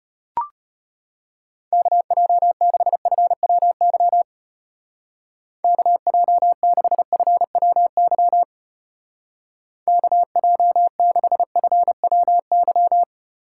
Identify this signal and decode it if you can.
Morse code, heard as a keyed tone, about 32 wpm, 700 Hz. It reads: KJ6FWY KJ6FWY KJ6FWY